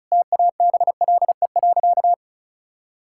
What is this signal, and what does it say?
Morse code, heard as a keyed tone, about 36 wpm, 700 Hz. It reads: TABLE.
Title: Sentences from Top 1600 Words 35wpm